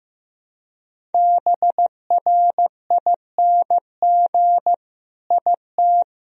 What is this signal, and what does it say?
Morse code, heard as a keyed tone, about 15 wpm, 700 Hz. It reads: BRING IT
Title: Sentences from Top 1000 Words - Review Part 4 - 15wpm